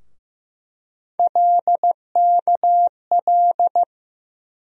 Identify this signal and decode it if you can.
Morse code, heard as a keyed tone, about 15 wpm, 700 Hz. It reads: LKL
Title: USA Call Sign Postfixes - 3 Letters 15wpm